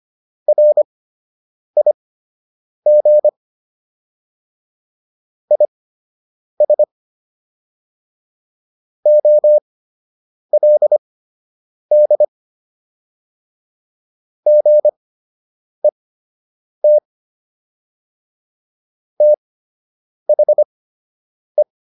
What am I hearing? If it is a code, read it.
Morse code, heard as a keyed tone, about 25 wpm, spaced as for 8 wpm, 600 Hz. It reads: RIG IS OLD GET THE